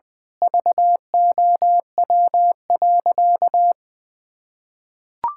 Morse code, heard as a keyed tone, about 20 wpm, 700 Hz. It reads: VOW.
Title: Pangram Challenge v2 - 20wpm